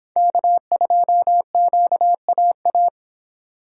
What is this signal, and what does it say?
Morse code, heard as a keyed tone, about 25 wpm, 700 Hz. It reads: K2QAA